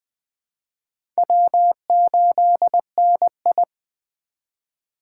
Morse code, heard as a keyed tone, about 20 wpm, 700 Hz. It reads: W8NI
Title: 2023 POTA Activator Call Signs - 20wpm